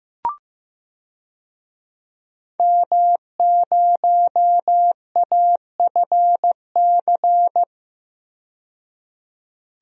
Morse code, heard as a keyed tone, about 15 wpm, 700 Hz. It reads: M0AFC
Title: DXCC-100 Call Signs - 5 Characters - 15wpm